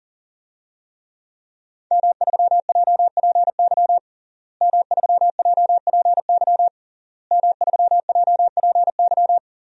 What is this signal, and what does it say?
Morse code, heard as a keyed tone, about 40 wpm, 700 Hz. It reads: M3JPY M3JPY M3JPY
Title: DXCC-100 Call Signs - 5 Characters - Repeated 3x - 40wpm